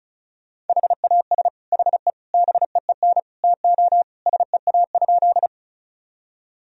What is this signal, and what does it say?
Morse code, heard as a keyed tone, about 36 wpm, 700 Hz. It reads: HAS HE BEEN TO SEA?
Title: Sentences from Top 1000 Words - Review Part 1 - 35wpm